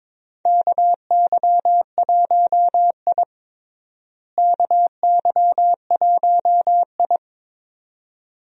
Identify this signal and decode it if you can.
Morse code, heard as a keyed tone, about 22 wpm, 700 Hz. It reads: KY1I KY1I
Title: USA Call Signs - 4 Characters - Repeated 2x - 22wpm